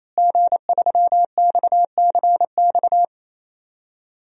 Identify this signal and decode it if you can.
Morse code, heard as a keyed tone, about 28 wpm, 700 Hz. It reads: G3XCX
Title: DXCC-100 Call Signs - 5 Characters - Repeated 3x - 28wpm